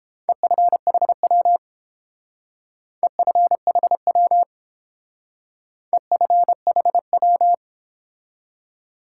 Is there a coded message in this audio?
Yes: Morse code from 0.3 to 7.5 s, heard as a keyed tone, about 30 wpm, 700 Hz. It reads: EFHW EFHW EFHW